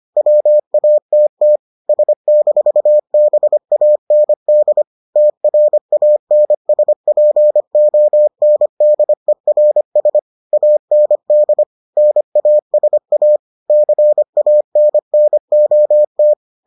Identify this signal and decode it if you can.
Morse code, heard as a keyed tone, about 25 wpm, 600 Hz. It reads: WATT S-BAND TRANSPONDERS AND NASA CANNOT